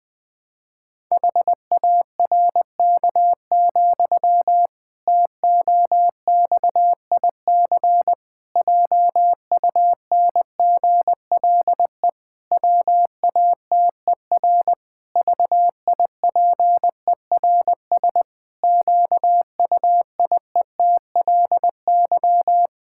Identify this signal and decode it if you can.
Morse code, heard as a keyed tone, about 19 wpm, 700 Hz. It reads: HARK, TOXIC JUNGLE WATER VIPERS QUIETLY
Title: Pangram Challenge v2 - 20wpm